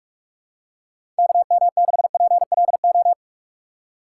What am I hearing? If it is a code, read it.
Morse code, heard as a keyed tone, about 46 wpm, 700 Hz. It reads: KM6PLO
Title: USA Call Signs - 6 Characters - 2x Character Spacing - 45wpm